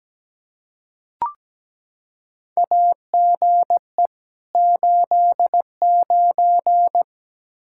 Morse code, heard as a keyed tone, about 17 wpm, 700 Hz. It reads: AGE 89
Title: K1USN SST QSO Practice 17wpm